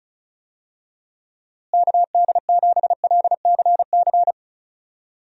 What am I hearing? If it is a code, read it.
Morse code, heard as a keyed tone, about 36 wpm, 700 Hz. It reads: KD7LCC
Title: USA Call Signs - 6 Characters - Repeated 2x - 35wpm